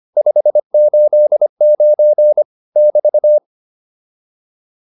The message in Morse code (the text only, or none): 589 <BT>